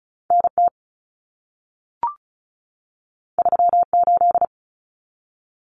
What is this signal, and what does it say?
Morse code, heard as a keyed tone, about 35 wpm, 700 Hz. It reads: NT 38
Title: Common Cut Numbers - 2-Digit 35wpm